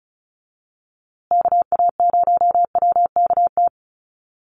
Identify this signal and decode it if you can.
Morse code, heard as a keyed tone, about 34 wpm, 700 Hz. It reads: KA0WKT
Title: USA Call Signs Repeated 2x 35wpm